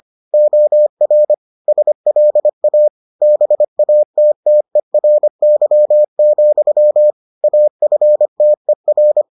Morse code, heard as a keyed tone, about 25 wpm, 600 Hz. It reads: OR SLA BATTERY, AFTER